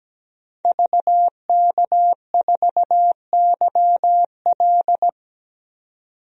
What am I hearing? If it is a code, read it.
Morse code, heard as a keyed tone, about 17 wpm, 700 Hz. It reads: VK4YL
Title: DXCC-100 Call Signs - 5 Characters - 17wpm